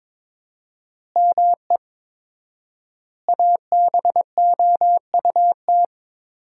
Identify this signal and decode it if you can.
Morse code, heard as a keyed tone, about 22 wpm, 700 Hz. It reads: ME ABOUT
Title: 4x Word-Spacing - Sentences from Top 200 Words - 22wpm